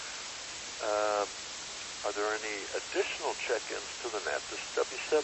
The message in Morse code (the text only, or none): none